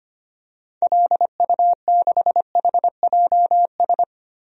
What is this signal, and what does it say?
Morse code, heard as a keyed tone, about 25 wpm, 700 Hz. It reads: LU6HJS
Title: DXCC-100 Call Signs - 6 Characters - Repeated 3x - 25wpm